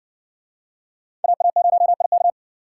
Morse code, heard as a keyed tone, about 60 wpm, 700 Hz. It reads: AA0IK